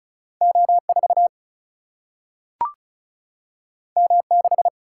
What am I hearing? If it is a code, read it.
Morse code, heard as a keyed tone, about 36 wpm, 700 Hz. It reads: O4 M6
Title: Single-Letter Single-Number 35wpm